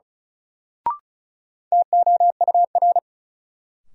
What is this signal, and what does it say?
Morse code, heard as a keyed tone, about 36 wpm, 700 Hz. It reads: TOUR
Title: Sentences from Top 1400 Words - New Word Drill - 35wpm